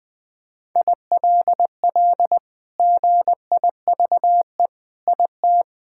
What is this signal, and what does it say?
Morse code, heard as a keyed tone, about 20 wpm, 700 Hz. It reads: ILL GIVE IT
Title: Sentences with 2-Word Phrases - Spoken Beginning - 20wpm